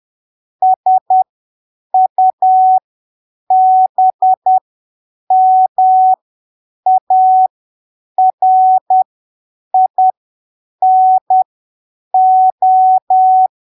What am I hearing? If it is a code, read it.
Morse code, heard as a keyed tone, about 10 wpm, spaced as for 7 wpm, 750 Hz. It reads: SUBMARINO
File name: arquivos / texto / texto 16 10ppm